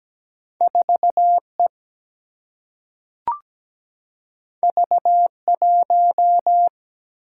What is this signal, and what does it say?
Morse code, heard as a keyed tone, about 17 wpm, 700 Hz. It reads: 4E V1
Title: Single-Letter Single-Number 17wpm